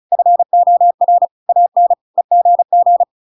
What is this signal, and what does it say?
Morse code, heard as a keyed tone, about 34 wpm, 700 Hz. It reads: FOR AN EGG
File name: Pinocchio / Pinocchio ch5 35 35 ik1hge